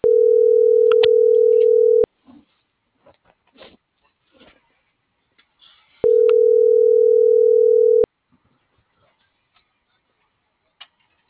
Ringback tone (North America), 440 + 480 Hz.
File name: CENTRAL TERRESTRE / out cris.f. - .50968